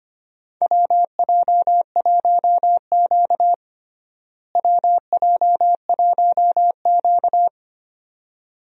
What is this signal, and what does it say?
Morse code, heard as a keyed tone, about 25 wpm, 700 Hz. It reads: WJ1Q WJ1Q
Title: USA Call Signs - 4 Characters - Repeated 2x - 25wpm